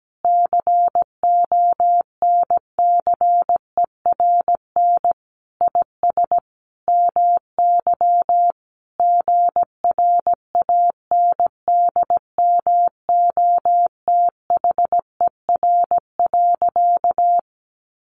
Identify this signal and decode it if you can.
Morse code, heard as a keyed tone, about 17 wpm, 700 Hz. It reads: CONCERN IS MY GRANDMOTHER.
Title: Sentences from Top 1200 Words 17wpm